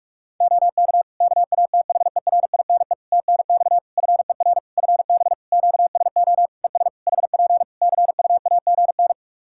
Morse code, heard as a keyed tone, about 46 wpm, 700 Hz. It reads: OK KATHERINE TNX FER FB QSO ES HP CUAGN
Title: K1USN SST QSO Practice 45wpm